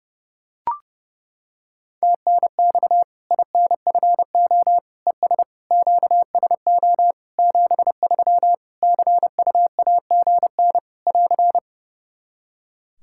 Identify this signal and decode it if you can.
Morse code, heard as a keyed tone, about 30 wpm, 700 Hz. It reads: TNX INFO ES QSO 73 CUAGN <AR>
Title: Practice QSO - Exchange 4 - Ending - 30wpm